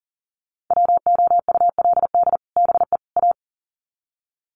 Morse code, heard as a keyed tone, about 40 wpm, 700 Hz. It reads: WOULD BE A
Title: Sets of 3 Words 40wpm